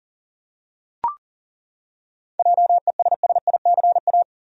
Morse code, heard as a keyed tone, about 40 wpm, 700 Hz. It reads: JESSICA
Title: USA Common Names - 40wpm